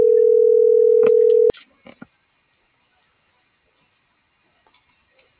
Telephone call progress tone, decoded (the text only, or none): ringback tone